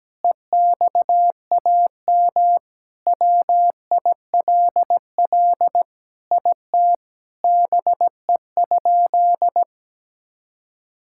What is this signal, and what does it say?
Morse code, heard as a keyed tone, about 17 wpm, 700 Hz. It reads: EXAM WILL IT BE?
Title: Sentences from Top 1000 Words - Review Part 5 - 17wpm